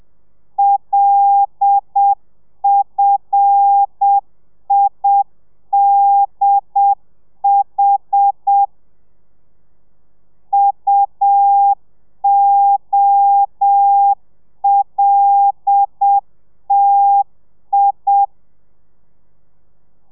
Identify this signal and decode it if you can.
Morse code, heard as a keyed tone, about 7 wpm, 800 Hz. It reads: LFIDH UOLTI
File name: telegrafia / i2tfj / Lez5 30